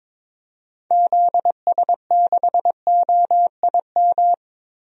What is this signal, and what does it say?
Morse code, heard as a keyed tone, about 22 wpm, 700 Hz. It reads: ZS6OIM
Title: DXCC-100 Call Signs - 6 Characters - 22wpm